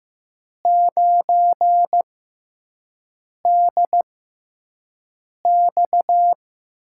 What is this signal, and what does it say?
Morse code, heard as a keyed tone, about 15 wpm, spaced as for 5 wpm, 700 Hz. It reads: 9DX
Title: USA Call Signs - 5 Characters - 6x Character Spacing - 15wpm